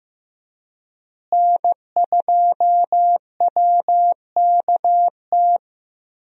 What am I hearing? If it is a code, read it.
Morse code, heard as a keyed tone, about 16 wpm, 700 Hz. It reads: N2WKT
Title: USA Call Signs - 5 Characters - Repeated 3x - 15wpm